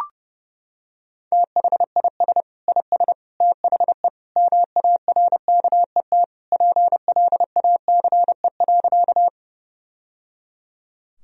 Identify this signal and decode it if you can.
Morse code, heard as a keyed tone, about 30 wpm, 700 Hz. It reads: THIS IS THE MARKET PLACE.
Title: Sentences from Top 1000 Words - Review Part 3 - 30wpm